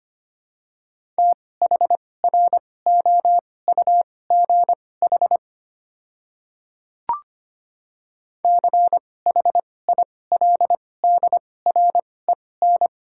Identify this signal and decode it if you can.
Morse code, heard as a keyed tone, about 25 wpm, spaced as for 18 wpm, 690 Hz. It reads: THROUGH CHILDREN
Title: 2x Character Spacing - Top 300 Words - 25wpm